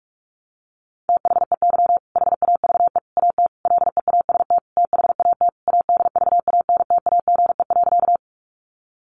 Morse code, heard as a keyed tone, about 44 wpm, 700 Hz. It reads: THEY HAVE AT LEAST THAT ADVANTAGE.
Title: Sentences from Top 900 Words 45wpm